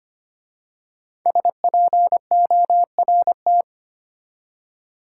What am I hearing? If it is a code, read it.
Morse code, heard as a keyed tone, about 25 wpm, 700 Hz. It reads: SPORT
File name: 5 Letter Words - First 3 Letters Given - 25wpm